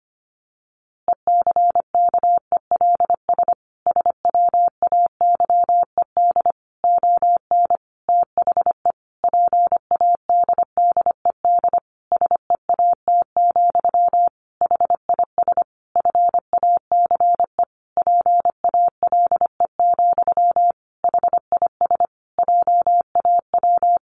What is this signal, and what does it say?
Morse code, heard as a keyed tone, about 25 wpm, 700 Hz. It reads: ECKELS SWAYED ON THE PADDED SEAT, HIS FACE PALE, HIS JAW
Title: A Sound Of Thunder 25wpm